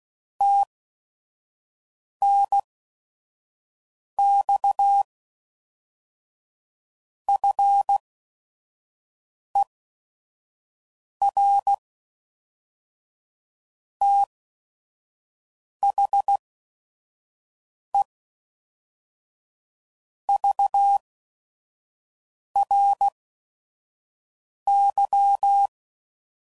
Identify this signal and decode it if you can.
Morse code, heard as a keyed tone, about 16 wpm, spaced as for 5 wpm, 800 Hz. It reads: TNX FER THE VRY
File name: CW Code Groups @5WPM / QSOs / kc7haa